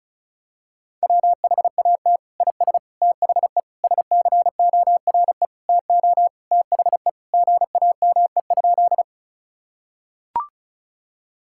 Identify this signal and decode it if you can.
Morse code, heard as a keyed tone, about 36 wpm, 700 Hz. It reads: WHAT IS THE SCORE TO THE GAME?
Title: Sentences from Top 1600 Words 35wpm